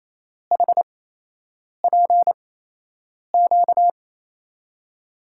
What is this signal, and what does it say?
Morse code, heard as a keyed tone, about 27 wpm, spaced as for 8 wpm, 700 Hz. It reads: HPQ
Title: USA Call Signs - 6 Characters - 8x Character Spacing - 28wpm